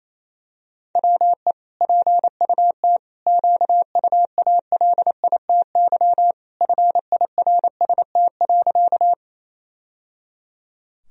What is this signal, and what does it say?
Morse code, heard as a keyed tone, about 27 wpm, 700 Hz. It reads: WE PUT QUALITY FIRST.
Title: Sentences from Top 1000 Words - Review Part 4 - 28wpm